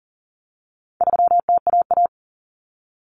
Morse code, heard as a keyed tone, about 37 wpm, 700 Hz. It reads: 3TAA
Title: Common Cut Numbers - 4-Digit 40wpm